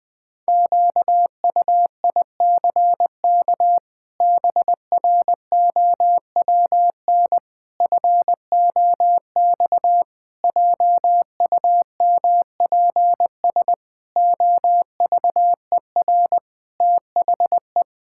Morse code, heard as a keyed tone, about 21 wpm, 700 Hz. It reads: QUICK BROWN FOX JUMPS OVER THE
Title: Sets of 8 Words - 20wpm